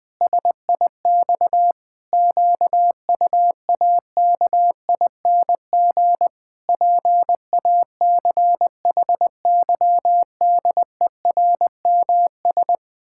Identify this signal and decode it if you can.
Morse code, heard as a keyed tone, about 21 wpm, 700 Hz. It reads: SIX QUAKING PACHYDERMS
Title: Pangram Challenge v2 - 20wpm